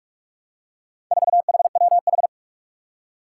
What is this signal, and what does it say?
Morse code, heard as a keyed tone, about 46 wpm, 700 Hz. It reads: 4HWH
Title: Four Challenging Letter-Number Combinations 45wpm